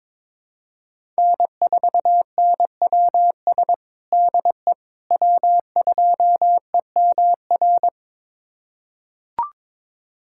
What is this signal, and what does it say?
Morse code, heard as a keyed tone, about 22 wpm, 700 Hz. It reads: N4NWS DE W2EMR
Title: USA Callsign DE Callsign - 5 Characters - 22wpm